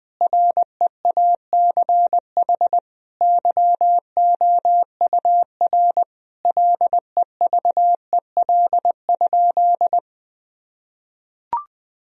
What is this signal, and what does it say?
Morse code, heard as a keyed tone, about 20 wpm, 700 Hz. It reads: REACH YOUR LEVEL?
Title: Sentences from Top 1000 Words - Review Part 4 - 20wpm